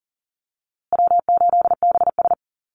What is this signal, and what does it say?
Morse code, heard as a keyed tone, about 40 wpm, 700 Hz. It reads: W8BS
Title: USA Call Signs Repeated 2x 40wpm